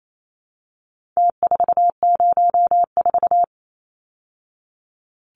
Morse code, heard as a keyed tone, about 28 wpm, 700 Hz. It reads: T404
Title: Numbers - Contest Style - 4-Digit 28wpm